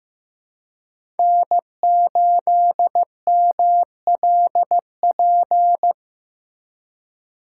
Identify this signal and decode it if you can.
Morse code, heard as a keyed tone, about 16 wpm, 700 Hz. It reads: N8MLP
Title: USA Call Signs - 5 Characters - 2x Character Spacing - 15wpm